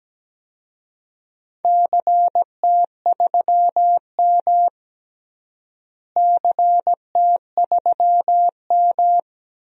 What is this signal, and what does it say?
Morse code, heard as a keyed tone, about 17 wpm, 700 Hz. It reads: CT3M CT3M
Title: DXCC-100 Call Signs - 4 Characters - Repeated 2x - 17wpm